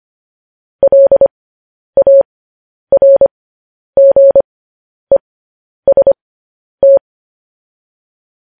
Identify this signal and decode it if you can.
Morse code, heard as a keyed tone, about 25 wpm, spaced as for 10 wpm, 550 Hz. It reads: LARGEST
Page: news-headlines-in-morse-code-at-10-wpm